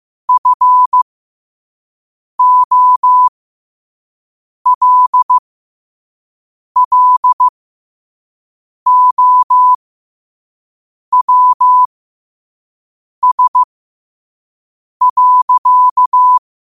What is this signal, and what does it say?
Morse code, heard as a keyed tone, about 15 wpm, spaced as for 5 wpm, 1000 Hz. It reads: FOLLOWS.
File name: qotdfiles / 5wpm